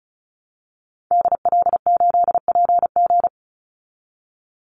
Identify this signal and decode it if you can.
Morse code, heard as a keyed tone, about 34 wpm, 700 Hz. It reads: DL8PG
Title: Call Signs Repeated 2x - 35wpm